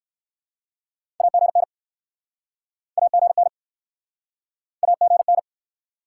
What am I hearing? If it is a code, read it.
Morse code, heard as a keyed tone, about 63 wpm, 700 Hz. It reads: AGN AGN AGN